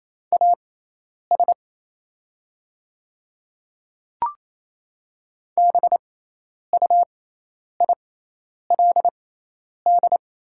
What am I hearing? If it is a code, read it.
Morse code, heard as a keyed tone, about 28 wpm, spaced as for 10 wpm, 700 Hz. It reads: AS BUILD
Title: 6x Character Spacing - Top 200 Words - 28wpm